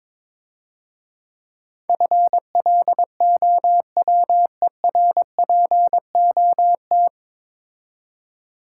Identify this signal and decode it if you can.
Morse code, heard as a keyed tone, about 22 wpm, 700 Hz. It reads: FLOWERPOT
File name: Easy Words - Speed-Racing with Hints - 22wpm